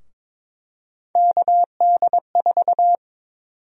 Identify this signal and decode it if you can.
Morse code, heard as a keyed tone, about 22 wpm, 700 Hz. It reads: KD4
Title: USA Call Sign Prefixes - 22wpm